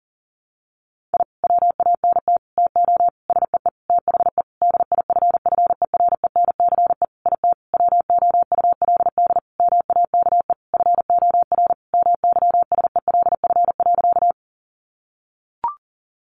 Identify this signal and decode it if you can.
Morse code, heard as a keyed tone, about 37 wpm, 700 Hz. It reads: I WANT TO SEE THE DIFFERENCE IT WOULD MAKE FOR MYSELF.
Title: Sentences from Top 1300 Words 40wpm